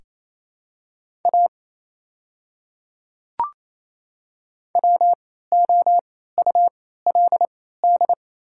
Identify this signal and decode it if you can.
Morse code, heard as a keyed tone, about 28 wpm, spaced as for 16 wpm, 700 Hz. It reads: A WOULD
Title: 3x Character Spacing - Top 100 Words - 28wpm